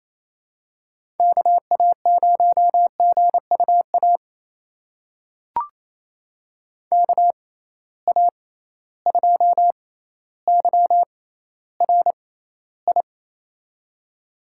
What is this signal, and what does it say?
Morse code, heard as a keyed tone, about 28 wpm, spaced as for 14 wpm, 700 Hz. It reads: KA0GUA KA2YRI
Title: USA Call Signs - 6 Characters - 6x Character Spacing - 28wpm